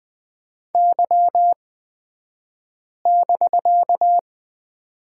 Morse code, heard as a keyed tone, about 20 wpm, spaced as for 5 wpm, 700 Hz. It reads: Y<BK>
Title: 249 Lesson 40d - BK - Two Characters 20wpm